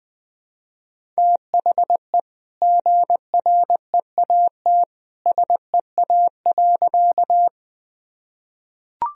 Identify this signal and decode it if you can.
Morse code, heard as a keyed tone, about 20 wpm, 700 Hz. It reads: THE GREAT SEA.
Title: Sentences from Top 1000 Words - Review Part 1 - 20wpm